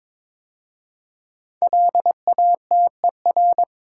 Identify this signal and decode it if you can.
Morse code, heard as a keyed tone, about 22 wpm, 700 Hz. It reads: LATER
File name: 5 Letter Words - First 3 Letters Given - 22wpm